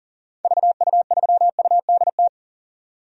Morse code, heard as a keyed tone, about 40 wpm, 700 Hz. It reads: VU3UDT
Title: DXCC-100 Call Signs - 6 Characters - 40wpm